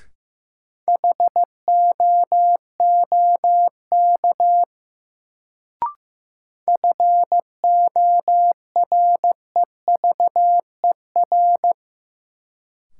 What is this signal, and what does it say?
Morse code, heard as a keyed tone, about 15 wpm, 700 Hz. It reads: HOOK FOREVER
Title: Sentences from Top 1600 Words - New Word Drill Alt - 15wpm